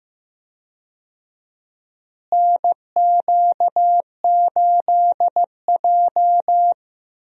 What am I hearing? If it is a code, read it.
Morse code, heard as a keyed tone, about 15 wpm, 700 Hz. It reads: NQ8J
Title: USA Call Signs - 4 Characters - Repeated 2x - 15wpm